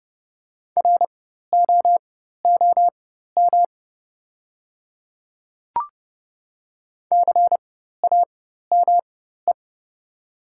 Morse code, heard as a keyed tone, about 30 wpm, spaced as for 14 wpm, 700 Hz. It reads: ROOM CAME
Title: 4x Character Spacing - Top 300 Words - 30wpm